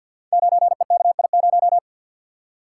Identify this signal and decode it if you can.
Morse code, heard as a keyed tone, about 47 wpm, 700 Hz. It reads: 9EKI0
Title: Code Groups - 5 Letter:Numbers - 50wpm